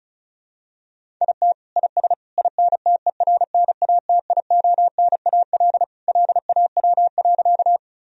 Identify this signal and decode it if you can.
Morse code, heard as a keyed tone, about 35 wpm, 700 Hz. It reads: IT IS INTERNATIONAL LAW.
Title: Sentences from Top 1000 Words - Review Part 4 - 35wpm